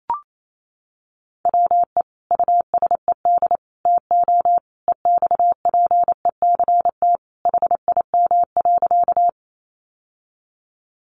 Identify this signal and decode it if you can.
Morse code, heard as a keyed tone, about 28 wpm, 700 Hz. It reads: WE USED TO EXPECT HIM.
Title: Sentences from Top 1100 Words 28wpm